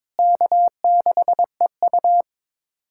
Morse code, heard as a keyed tone, about 22 wpm, 700 Hz. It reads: K6EU
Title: USA Callsign DE Callsign - 4 Characters - 22wpm